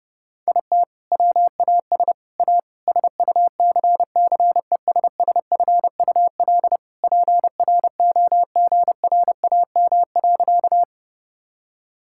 Morse code, heard as a keyed tone, about 30 wpm, 700 Hz. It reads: IT WAS A SUCCESSFUL PROGRAM.
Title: Sentences from Top 1000 Words - Review Part 5 - 30wpm